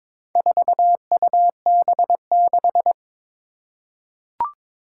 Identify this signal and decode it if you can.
Morse code, heard as a keyed tone, about 22 wpm, 700 Hz. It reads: 4UB6
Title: Four Challenging Letter-Number Combinations 22wpm